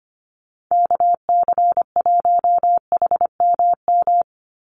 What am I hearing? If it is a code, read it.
Morse code, heard as a keyed tone, about 25 wpm, 700 Hz. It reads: KC1HMM
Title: USA Call Signs 25wpm